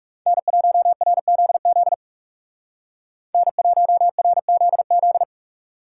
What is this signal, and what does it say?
Morse code, heard as a keyed tone, about 39 wpm, 700 Hz. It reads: N1RZZ N1RZZ